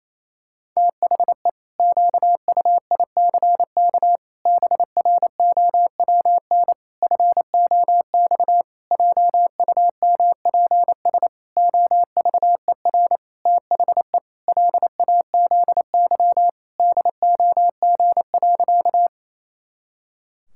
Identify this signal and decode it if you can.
Morse code, heard as a keyed tone, about 27 wpm, 700 Hz. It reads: THE QUICK BROWN FOX JUMPS OVER THE LAZY DOG.
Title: Pangram Challenge v2 - 28wpm